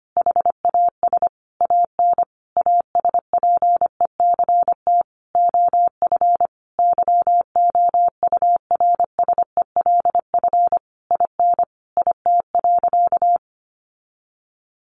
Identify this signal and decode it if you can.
Morse code, heard as a keyed tone, about 25 wpm, 700 Hz. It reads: HAS AN ASPECT OF YOURSELF IN IT.
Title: Sentences from Top 1300 Words 25wpm